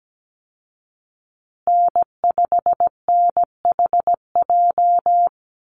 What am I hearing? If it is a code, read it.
Morse code, heard as a keyed tone, about 17 wpm, 700 Hz. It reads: N5NHJ
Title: Call Signs Repeated 2x - 17wpm